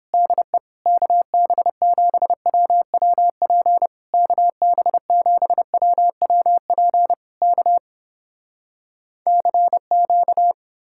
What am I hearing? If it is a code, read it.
Morse code, heard as a keyed tone, about 31 wpm, 700 Hz. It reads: DE KB7WWP KB7WWP K CQ